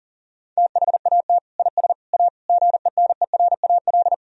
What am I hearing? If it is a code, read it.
Morse code, heard as a keyed tone, about 37 wpm, 700 Hz. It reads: THAT IS A GENERAL
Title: Sentences from Top 1000 Words - Review Part 3 - 40wpm